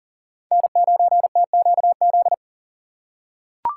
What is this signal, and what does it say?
Morse code, heard as a keyed tone, about 40 wpm, 700 Hz. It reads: N9TQZ
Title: USA Call Signs - 5 Characters - 4x Character Spacing - 40wpm